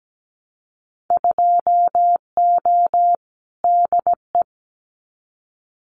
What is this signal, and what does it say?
Morse code, heard as a keyed tone, about 17 wpm, 700 Hz. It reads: 2O DE
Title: Winter Field Day Exchanges Encore 17wpm